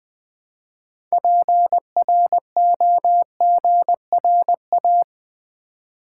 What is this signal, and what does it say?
Morse code, heard as a keyed tone, about 20 wpm, 700 Hz. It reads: PROGRA
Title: Build A Word - 3 Letters At A Time - 9-Letter Words - 20wpm